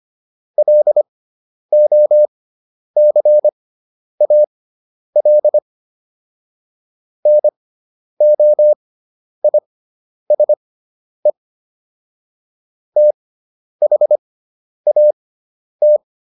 Morse code, heard as a keyed tone, about 25 wpm, spaced as for 10 wpm, 600 Hz. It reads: LOCAL NOISE THAT